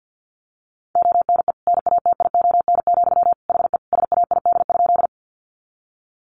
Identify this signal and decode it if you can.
Morse code, heard as a keyed tone, about 50 wpm, 700 Hz. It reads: ONE NATION, HE SAID?
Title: Sentences from Top 900 Words 50wpm